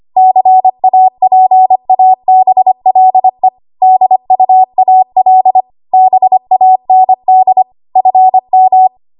Morse code, heard as a keyed tone, about 25 wpm, 750 Hz. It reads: CAPABLE DUAL BAND FM
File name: Archive / 25 WPM / 25WPM